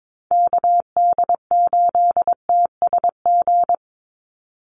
Morse code, heard as a keyed tone, about 22 wpm, 700 Hz. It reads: KD8TSG
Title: USA Call Signs 22wpm